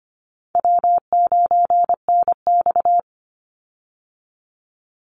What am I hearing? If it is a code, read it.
Morse code, heard as a keyed tone, about 25 wpm, 700 Hz. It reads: W9NX
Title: Easy Call Signs - 25wpm